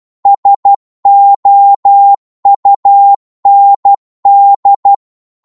Morse code, heard as a keyed tone, about 12 wpm, 800 Hz. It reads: SOUND
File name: Morse / psg-12wpm